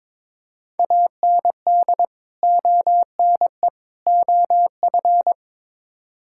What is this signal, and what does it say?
Morse code, heard as a keyed tone, about 22 wpm, 700 Hz. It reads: AND ONE OF
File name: practice / 22wpm / Build A Phrase - 5-Word Phrases - 22wpm